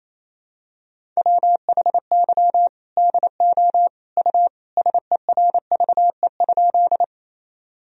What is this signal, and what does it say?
Morse code, heard as a keyed tone, about 28 wpm, 700 Hz. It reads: WHY DO U SERVE?
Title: Sentences from Top 400 Words and Abbreviations 28wpm